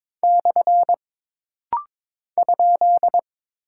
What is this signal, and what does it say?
Morse code, heard as a keyed tone, about 22 wpm, 700 Hz. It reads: / ?
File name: practice / 22wpm / Single Punctuation-Prosign - Rapid-Fire - 22wpm